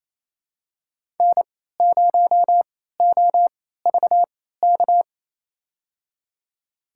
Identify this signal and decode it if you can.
Morse code, heard as a keyed tone, about 28 wpm, spaced as for 16 wpm, 700 Hz. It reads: N0OVK
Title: USA Call Signs - 3x Character Spacing - 28wpm